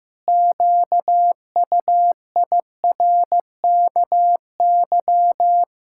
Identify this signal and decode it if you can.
Morse code, heard as a keyed tone, about 15 wpm, 700 Hz. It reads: QUIRKY
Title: Pangram Challenge v2 - 15wpm